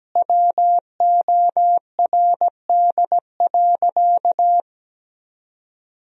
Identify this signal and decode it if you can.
Morse code, heard as a keyed tone, about 17 wpm, 700 Hz. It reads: WORD.
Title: Sentences from Top 1000 Words - Review Part 3 - 17wpm